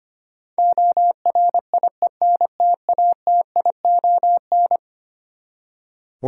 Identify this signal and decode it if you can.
Morse code, heard as a keyed tone, about 25 wpm, 700 Hz. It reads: ORIENTATION
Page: morse-code-ninja-practice-25wpm